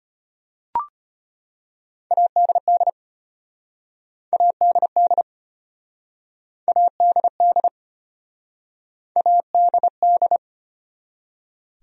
Morse code, heard as a keyed tone, about 33 wpm, 700 Hz. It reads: ADD ADD ADD ADD